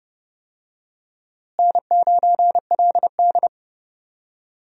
Morse code, heard as a keyed tone, about 30 wpm, 700 Hz. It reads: N9LD
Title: USA Call Signs - 4 Characters - Repeated 3x - 30wpm